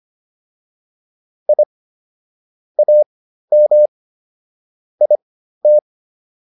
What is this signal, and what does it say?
Morse code, heard as a keyed tone, about 25 wpm, spaced as for 13 wpm, 600 Hz. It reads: I AM IT